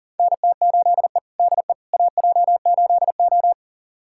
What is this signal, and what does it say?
Morse code, heard as a keyed tone, about 40 wpm, 700 Hz. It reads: NT8E DE AJ8O
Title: USA Callsign DE Callsign - 4 Characters - 40wpm